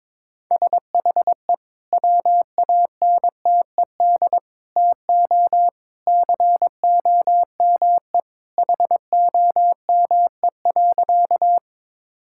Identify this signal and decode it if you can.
Morse code, heard as a keyed tone, about 22 wpm, 700 Hz. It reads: SHE WANTED TO COME HOME.
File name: practice / 22wpm / Sentences with 2-Word Phrases - 22wpm